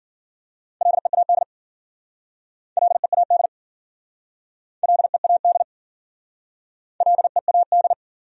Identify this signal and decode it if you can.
Morse code, heard as a keyed tone, about 55 wpm, 700 Hz. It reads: LEAD LEAD LEAD LEAD